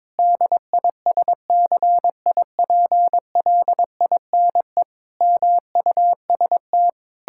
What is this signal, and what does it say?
Morse code, heard as a keyed tone, about 22 wpm, 700 Hz. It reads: DISCIPLINE MUST
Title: Sentences from Top 1000 Words - Review Part 6 - 22wpm